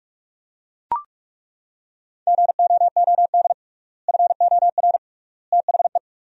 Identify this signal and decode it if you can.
Morse code, heard as a keyed tone, about 42 wpm, 700 Hz. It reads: GOOD FOR THE
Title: 3x Word-Spacing - Sentences from Top 300 Words - 45wpm